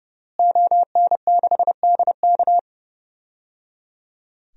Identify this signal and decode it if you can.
Morse code, heard as a keyed tone, about 30 wpm, 700 Hz. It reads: ON6DK
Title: DXCC-100 Call Signs - 5 Characters - Repeated 3x - 30wpm